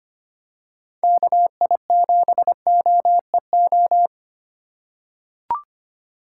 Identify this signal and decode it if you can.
Morse code, heard as a keyed tone, about 25 wpm, 700 Hz. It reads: KI7OEO
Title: USA Call Signs - 6 Characters - 4x Character Spacing - 25wpm